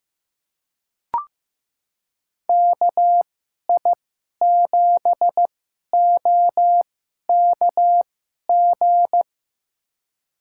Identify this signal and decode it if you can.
Morse code, heard as a keyed tone, about 15 wpm, spaced as for 11 wpm, 700 Hz. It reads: KI7OKG
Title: USA Call Signs - 6 Characters - 2x Character Spacing - 15wpm